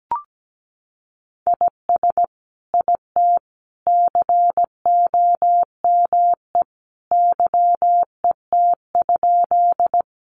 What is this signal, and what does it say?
Morse code, heard as a keyed tone, about 17 wpm, 700 Hz. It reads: IS IT COME YET?
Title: Sentences from Top 500 Words 17wpm